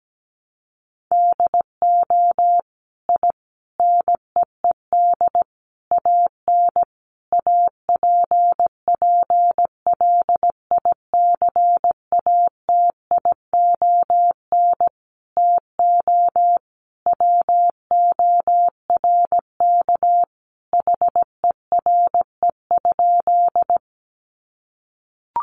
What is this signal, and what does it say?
Morse code, heard as a keyed tone, about 17 wpm, 700 Hz. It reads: DO I NEED AN APPLICATION TO WORK HERE?
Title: Sentences from Top 1200 Words 17wpm